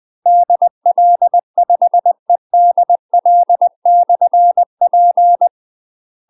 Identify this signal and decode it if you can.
Morse code, heard as a keyed tone, about 20 wpm, 700 Hz. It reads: DL5EDL/P
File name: UK MASTER QSOs TRAINING/20WPM